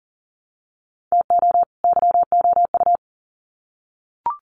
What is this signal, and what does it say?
Morse code, heard as a keyed tone, about 40 wpm, 700 Hz. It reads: TO YOU
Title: Sets of 2 Words 40wpm